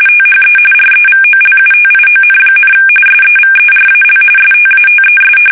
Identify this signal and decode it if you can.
RTTY, about 45.45 baud. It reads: THE REVOLUTION, SEVERAL STATES
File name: fsk850 / usnavy1